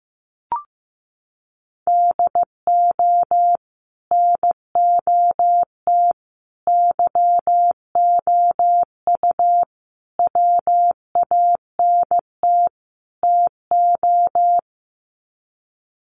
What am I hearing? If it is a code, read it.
Morse code, heard as a keyed tone, about 15 wpm, 700 Hz. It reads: DO NOT YOU WANT TO
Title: Sets of 5 Words Encore2 15wpm